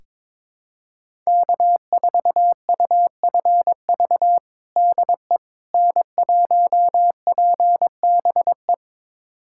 Morse code, heard as a keyed tone, about 21 wpm, 700 Hz. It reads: K4UFV DE N1PBE
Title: USA Callsign DE Callsign - 5 Characters - 22wpm